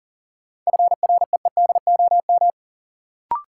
Morse code, heard as a keyed tone, about 40 wpm, 700 Hz. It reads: FREEDOM
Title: Sentences from Top 800 Words - New Word Drill - 40wpm